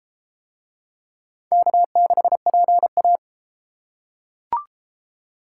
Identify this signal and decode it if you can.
Morse code, heard as a keyed tone, about 32 wpm, 700 Hz. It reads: K6PA